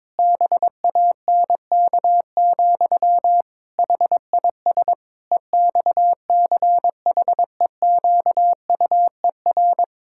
Morse code, heard as a keyed tone, about 21 wpm, 700 Hz. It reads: BANK, HIS EXCHEQUER
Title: Pangram Challenge v2 - 22wpm